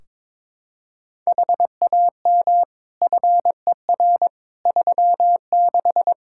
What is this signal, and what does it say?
Morse code, heard as a keyed tone, about 22 wpm, 700 Hz. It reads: HAM FER 36
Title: Practice QSO - Exchange 3 - More Info - 22wpm